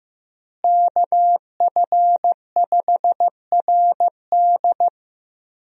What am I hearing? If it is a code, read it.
Morse code, heard as a keyed tone, about 15 wpm, 700 Hz. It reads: KF5RD
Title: USA Call Signs - 5 Characters - Repeated 2x - 15wpm